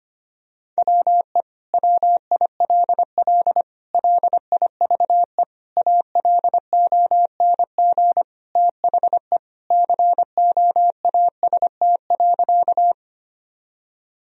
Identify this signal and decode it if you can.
Morse code, heard as a keyed tone, about 25 wpm, 700 Hz. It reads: WE WILL LIVE ALONG THE COAST.
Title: Sentences from Top 1000 Words - Review Part 4 - 25wpm